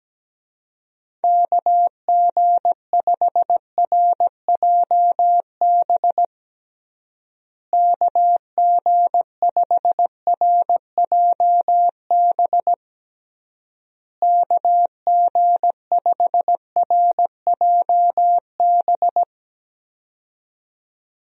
Morse code, heard as a keyed tone, about 17 wpm, 700 Hz. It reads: KG5RJB KG5RJB KG5RJB
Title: USA Call Signs - 6 Characters - Repeated 3x - 17wpm